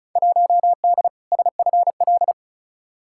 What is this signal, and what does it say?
Morse code, heard as a keyed tone, about 34 wpm, 700 Hz. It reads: 1D SFL
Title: Field Day Exchanges - 35wpm